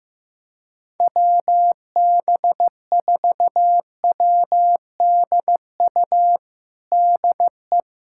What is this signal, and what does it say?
Morse code, heard as a keyed tone, about 15 wpm, 700 Hz. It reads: WB4WDU DE
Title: USA Callsign DE Callsign - 6 Characters - 15wpm